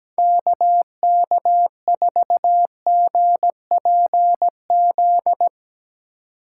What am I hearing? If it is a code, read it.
Morse code, heard as a keyed tone, about 17 wpm, 700 Hz. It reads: KK4GPZ
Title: USA Call Signs - 6 Characters - 3x Character Spacing - 17wpm